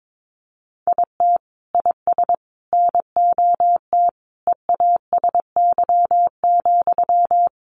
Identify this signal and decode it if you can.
Morse code, heard as a keyed tone, about 22 wpm, 700 Hz. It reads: IT IS NOT EASY,
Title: Sentences from Top 700 Words 22wpm